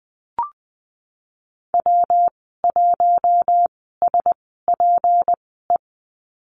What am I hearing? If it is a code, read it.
Morse code, heard as a keyed tone, about 19 wpm, spaced as for 14 wpm, 700 Hz. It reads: W1SPE
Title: 154 Lesson 25g - Letter p - Call Signs 20wpm